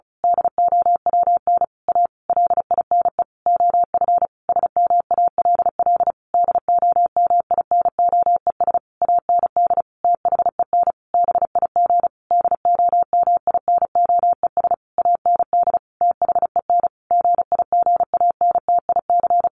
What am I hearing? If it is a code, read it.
Morse code, heard as a keyed tone, about 34 wpm, 700 Hz. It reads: DOWN A LINE OF SMALL DOMINOES AND THEN BIG DOMINOES AND THEN GIGANTIC
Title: A Sound Of Thunder 35wpm